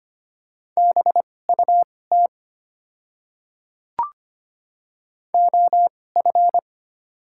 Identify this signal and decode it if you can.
Morse code, heard as a keyed tone, about 25 wpm, spaced as for 18 wpm, 700 Hz. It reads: BUT OF
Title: 2x Character Spacing - Top 200 Words - 25wpm